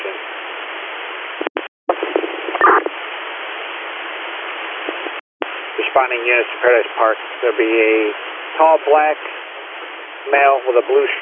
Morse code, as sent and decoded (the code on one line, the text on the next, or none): none